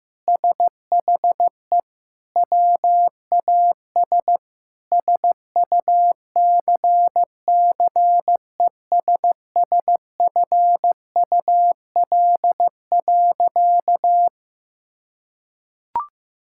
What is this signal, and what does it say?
Morse code, heard as a keyed tone, about 15 wpm, 700 Hz. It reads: SHE WAS SUCCESSFUL.
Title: Sentences from Top 1000 Words - Review Part 6 - 15wpm